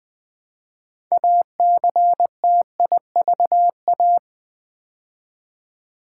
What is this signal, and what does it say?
Morse code, heard as a keyed tone, about 20 wpm, 700 Hz. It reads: ACTIVA
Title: Build A Word - 2 Letters At A Time - 8-Letter Words - 20wpm